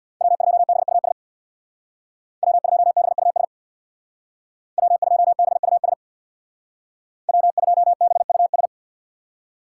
Morse code, heard as a keyed tone, about 57 wpm, 700 Hz. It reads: W2BUS W2BUS W2BUS W2BUS
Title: Speed Racing - USA 1x3 Call Signs 50wpm